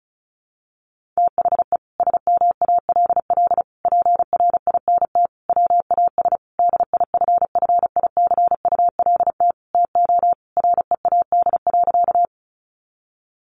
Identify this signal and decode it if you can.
Morse code, heard as a keyed tone, about 34 wpm, 700 Hz. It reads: THE SMALL PRINT WAS DIFFICULT TO READ.
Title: Sentences from Top 1400 Words 35wpm